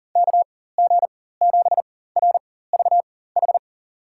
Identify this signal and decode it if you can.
Morse code, heard as a keyed tone, about 40 wpm, spaced as for 19 wpm, 700 Hz. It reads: KG7RVH
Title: USA Call Signs - 6 Characters - 4x Character Spacing - 40wpm